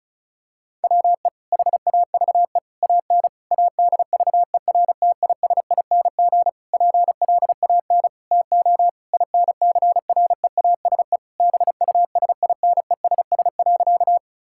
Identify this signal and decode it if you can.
Morse code, heard as a keyed tone, about 36 wpm, 690 Hz. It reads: WE HAVE AN ADVERTISING PLAN TO INCREASE BUSINESS.
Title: Sentences from Top 1000 Words - Review Part 6 - 35wpm